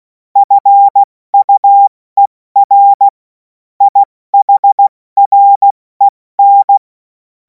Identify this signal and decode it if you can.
Morse code, heard as a keyed tone, about 16 wpm, spaced as for 14 wpm, 800 Hz. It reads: FUER IHREN
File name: Morsekurs / Witze / Witz 03. Februar 2026 70 F80